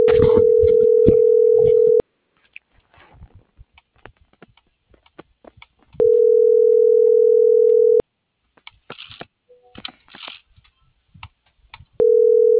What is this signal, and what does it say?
Ringback tone (North America), 440 + 480 Hz.